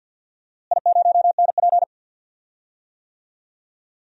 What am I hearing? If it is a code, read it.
Morse code, heard as a keyed tone, about 46 wpm, 700 Hz. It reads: I0NP
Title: DXCC-100 Call Signs - 4 Characters - Repeated 3x - 50wpm